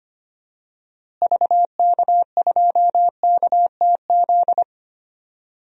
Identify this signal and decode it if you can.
Morse code, heard as a keyed tone, about 25 wpm, 700 Hz. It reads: VK2KTZ
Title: DXCC-100 Call Signs - 6 Characters - Repeated 2x - 25wpm